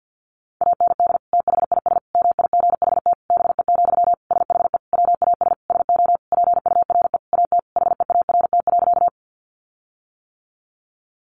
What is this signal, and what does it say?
Morse code, heard as a keyed tone, about 48 wpm, 700 Hz. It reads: AND THIS MIGHT BE, SHE WAS SO PURE AT HEART.
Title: Sentences from Top 1500 Words 50wpm